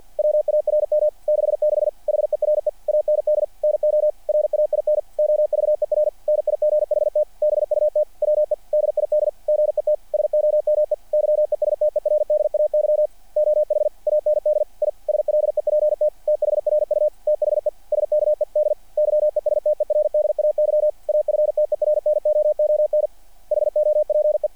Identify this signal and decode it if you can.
Morse code, heard as a keyed tone, about 48 wpm, 600 Hz. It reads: WARM 66 HERE AND NO RAIN OVER NIGHT BUT WE DID GET SOME YESTERDAY OH AND I SLEPT THRU THE SKED YESTERDAY AFTERNOON HOPE
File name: archived / CRSnet